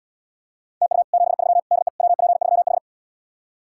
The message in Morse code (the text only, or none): TU 73 DE WP3H